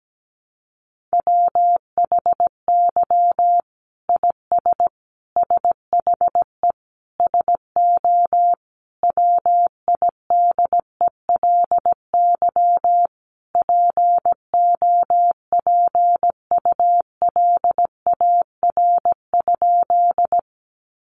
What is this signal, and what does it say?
Morse code, heard as a keyed tone, about 17 wpm, 700 Hz. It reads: WHY IS SHE SO WIDELY POPULAR?
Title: Sentences from Top 1200 Words 17wpm